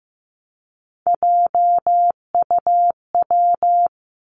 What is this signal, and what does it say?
Morse code, heard as a keyed tone, about 15 wpm, 700 Hz. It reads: JUW
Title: Three Challenging Letter-Number Combinations 15wpm